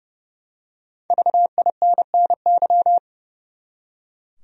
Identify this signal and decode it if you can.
Morse code, heard as a keyed tone, about 32 wpm, 700 Hz. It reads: VINNY
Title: USA Common Names - Max 5 Letters - 30wpm